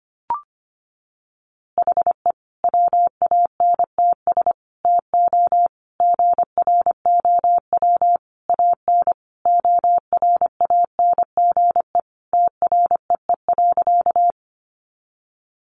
Morse code, heard as a keyed tone, about 25 wpm, 700 Hz. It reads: HE WANTS TO GROW AN ORANGE TREE.
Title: Easy Phrases 25wpm